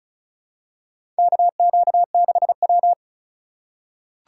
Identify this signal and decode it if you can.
Morse code, heard as a keyed tone, about 34 wpm, 700 Hz. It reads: KQ6W
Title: USA Call Signs - 4 Characters - 3x Character Spacing - 35wpm